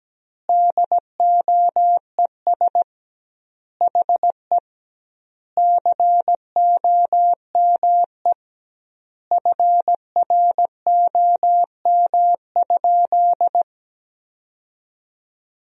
Morse code, heard as a keyed tone, about 17 wpm, 700 Hz. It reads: DOES HE COME FROM?
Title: 2x Word-Spacing - Sentences from Top 300 Words - 17wpm